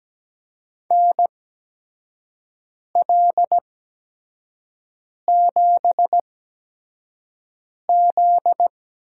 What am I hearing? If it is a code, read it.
Morse code, heard as a keyed tone, about 17 wpm, spaced as for 5 wpm, 700 Hz. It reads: NL7Z
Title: USA Call Signs - 8x Character Spacing - 17wpm